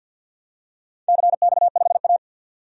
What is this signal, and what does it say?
Morse code, heard as a keyed tone, about 52 wpm, 700 Hz. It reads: CX5A
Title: DXCC-100 Call Signs - 4 Characters - 50wpm